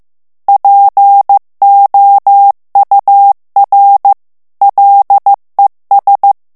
Morse code, heard as a keyed tone, about 15 wpm, 800 Hz. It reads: POUR LES
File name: morse mp3 / FRANCAIS TEXTE CLAIR 2 VITESSE 14 MOTS MINUTES